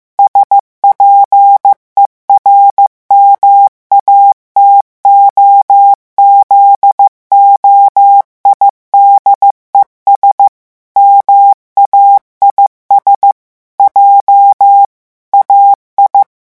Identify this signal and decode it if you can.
Morse code, heard as a keyed tone, about 15 wpm, 800 Hz. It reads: SPERMATOZOIDES MAIS J AI
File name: morse mp3 / revision9 citations humoristiques vitesse 14 mots minute